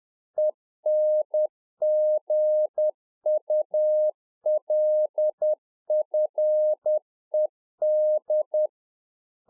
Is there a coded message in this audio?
Yes: Morse code from 0.4 to 8.7 s, heard as a keyed tone, about 10 wpm, 610 Hz. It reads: ENGULFED